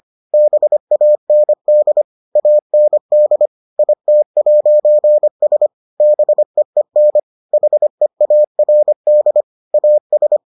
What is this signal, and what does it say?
Morse code, heard as a keyed tone, about 25 wpm, 600 Hz. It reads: BAND AND IT'S BEEN HEARD AS